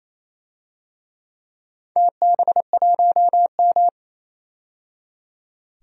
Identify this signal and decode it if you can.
Morse code, heard as a keyed tone, about 28 wpm, 700 Hz. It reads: TB1M
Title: DXCC-100 Call Signs - 4 Characters - 28wpm